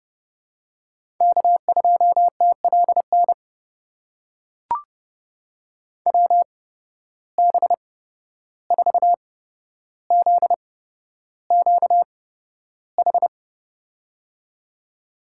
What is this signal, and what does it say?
Morse code, heard as a keyed tone, about 30 wpm, spaced as for 12 wpm, 700 Hz. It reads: K2TLN WB4ZQH
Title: USA Call Signs - 8x Character Spacing - 30wpm